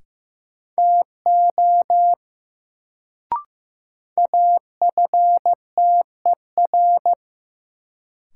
Morse code, heard as a keyed tone, about 15 wpm, 700 Hz. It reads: TO AFTER
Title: Sets of 2 Words - New Word Drill - 15wpm